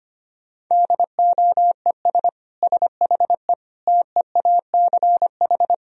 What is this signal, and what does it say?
Morse code, heard as a keyed tone, about 25 wpm, 700 Hz. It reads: DOES SHE TEACH
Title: Sentences from Top 1000 Words - Review Part 3 - 25wpm